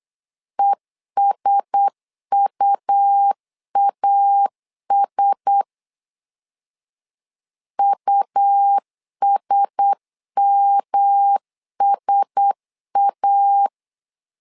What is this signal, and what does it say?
Morse code, heard as a keyed tone, about 8 wpm, 800 Hz. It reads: ESUAS USMSA
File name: LEZIONI / 4 Q1S 8wpm 2710